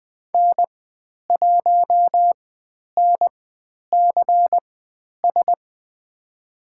Morse code, heard as a keyed tone, about 20 wpm, spaced as for 10 wpm, 700 Hz. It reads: N1NCS
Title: 095 Lesson 16g - Letter c - Call Signs 20wpm